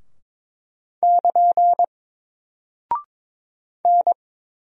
Morse code, heard as a keyed tone, about 22 wpm, 700 Hz. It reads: <KN> N
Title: Single Character (Letter:Number:Punctuation:Prosign) 22wpm